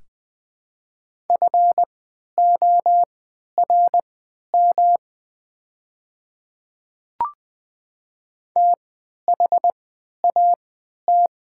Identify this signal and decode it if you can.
Morse code, heard as a keyed tone, about 20 wpm, spaced as for 11 wpm, 700 Hz. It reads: FORM THAT
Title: 3x Character Spacing - Top 300 Words - 20wpm